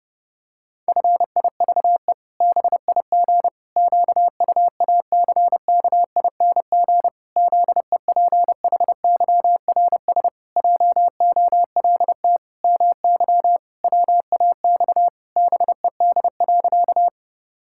Morse code, heard as a keyed tone, about 30 wpm, 700 Hz. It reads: FIVE BIG QUACKING ZEPHYRS JOLT MY WAX BED.
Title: Pangram Challenge v2 - 30wpm